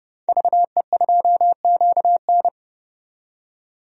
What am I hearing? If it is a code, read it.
Morse code, heard as a keyed tone, about 30 wpm, 700 Hz. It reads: VE2QN
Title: DXCC-100 Call Signs - 5 Characters - 30wpm